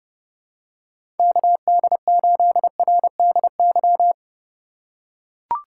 Morse code, heard as a keyed tone, about 30 wpm, 700 Hz. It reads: KD8RDY
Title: USA Call Signs - 6 Characters - 4x Character Spacing - 30wpm